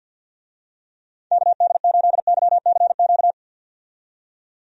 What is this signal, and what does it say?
Morse code, heard as a keyed tone, about 52 wpm, 700 Hz. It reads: KD8YCQ
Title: USA Call Signs - 6 Characters - 50wpm